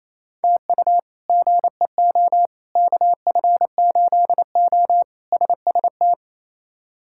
Morse code, heard as a keyed tone, about 28 wpm, 700 Hz. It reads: TU GEO KF8O SST
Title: K1USN SST QSO Practice - 28wpm